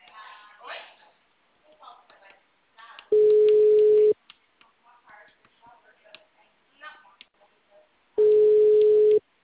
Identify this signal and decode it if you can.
Ringback tone (Europe, 425 Hz), with speech.